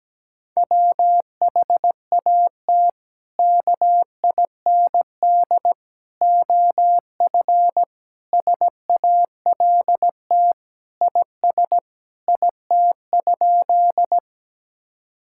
Morse code, heard as a keyed tone, about 17 wpm, 700 Hz. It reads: WHAT KIND OF SALT IS IT?
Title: Sentences from Top 1000 Words - Review Part 5 - 17wpm